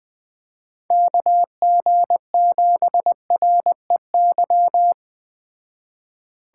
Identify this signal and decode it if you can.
Morse code, heard as a keyed tone, about 20 wpm, 700 Hz. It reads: KG7REY
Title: Speed Racing - USA 2x3 Call Signs 17wpm